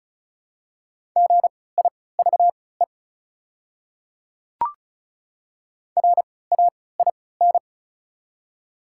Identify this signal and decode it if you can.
Morse code, heard as a keyed tone, about 36 wpm, spaced as for 20 wpm, 700 Hz. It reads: GIVE RAIN
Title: 3x Character Spacing - Top 300 Words - 35wpm